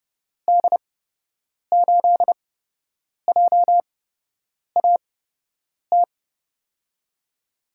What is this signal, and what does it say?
Morse code, heard as a keyed tone, about 31 wpm, spaced as for 8 wpm, 700 Hz. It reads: D8JAT
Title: USA Call Signs - 6 Characters - 8x Character Spacing - 30wpm